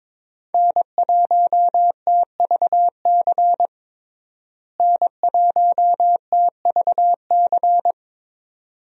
Morse code, heard as a keyed tone, about 22 wpm, 700 Hz. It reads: N1TVC N1TVC
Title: USA Call Signs - 5 Characters - Repeated 2x - 22wpm